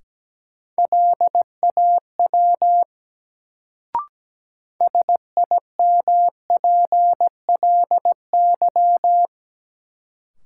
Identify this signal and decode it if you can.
Morse code, heard as a keyed tone, about 17 wpm, 700 Hz. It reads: LAW SIMPLY
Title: Sentences from Top 600 Words - New Word Drill - 17wpm